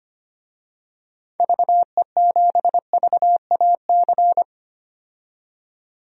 Morse code, heard as a keyed tone, about 25 wpm, 700 Hz. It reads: VE7VAC
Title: DXCC-100 Call Signs - 6 Characters - 25wpm